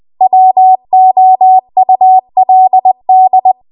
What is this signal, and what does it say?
Morse code, heard as a keyed tone, about 20 wpm, 750 Hz. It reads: WOULD
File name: Archive / 20 WPM / 20WPM